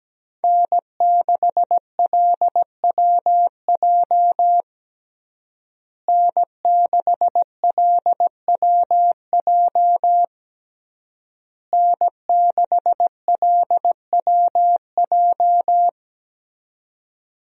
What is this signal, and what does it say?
Morse code, heard as a keyed tone, about 17 wpm, 700 Hz. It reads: N6LWJ N6LWJ N6LWJ
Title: USA Call Signs - 5 Characters - Repeated 3x - 17wpm